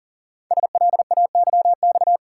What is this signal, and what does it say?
Morse code, heard as a keyed tone, about 42 wpm, 700 Hz. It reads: SLAYX
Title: Code Groups - 5 Letters - 40wpm